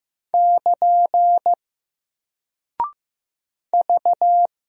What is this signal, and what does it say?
Morse code, heard as a keyed tone, about 15 wpm, 700 Hz. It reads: <KN> V
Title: Single Character (Letter:Number:Punctuation:Prosign) Alt 15wpm